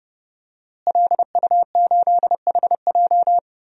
Morse code, heard as a keyed tone, about 30 wpm, 700 Hz. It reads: LU8HJ
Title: DXCC-100 Call Signs - 5 Characters - Repeated 3x - 30wpm